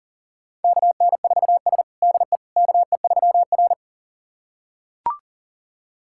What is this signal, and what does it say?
Morse code, heard as a keyed tone, about 40 wpm, 700 Hz. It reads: KN4S DE KE3R
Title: USA Callsign DE Callsign - 4 Characters - 40wpm